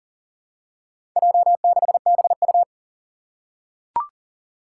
Morse code, heard as a keyed tone, about 40 wpm, 700 Hz. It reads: J6BU
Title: Four Challenging Letter-Number Combinations 40wpm